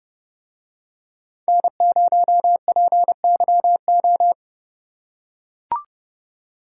Morse code, heard as a keyed tone, about 31 wpm, 700 Hz. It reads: N0PYO